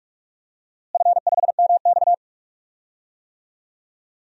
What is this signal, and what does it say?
Morse code, heard as a keyed tone, about 41 wpm, 700 Hz. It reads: F5MX
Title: DXCC-100 Call Signs - 4 Characters - Repeated 3x - 45wpm